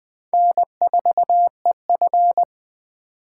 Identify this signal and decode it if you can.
Morse code, heard as a keyed tone, about 20 wpm, 700 Hz. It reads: N4EF
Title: USA Call Signs - 4x Character Spacing - 20wpm